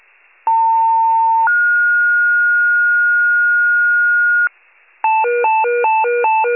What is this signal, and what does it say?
Two-tone page, heard as a one-second tone, then a three-second tone, about 900 then 1430 Hz, with noise. It is heard from 0.5 to 4.5 s.